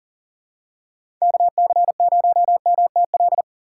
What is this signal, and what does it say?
Morse code, heard as a keyed tone, about 40 wpm, 700 Hz. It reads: KC0MTL
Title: USA Call Signs - 2x Character Spacing - 40wpm